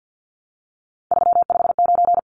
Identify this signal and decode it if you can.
Morse code, heard as a keyed tone, about 50 wpm, 700 Hz. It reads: 359
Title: Numbers - 3-Digit 50wpm